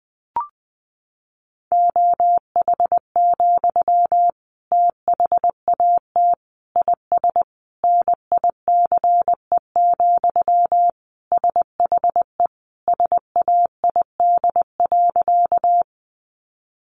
Morse code, heard as a keyed tone, about 19 wpm, 700 Hz. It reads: OH, THAT IS NICE, SHE SAID.
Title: Sentences from Top 1100 Words 20wpm